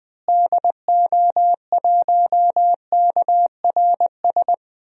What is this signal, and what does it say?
Morse code, heard as a keyed tone, about 20 wpm, 700 Hz. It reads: DO1KRS
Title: DXCC-100 Call Signs - 6 Characters - Repeated 3x - 20wpm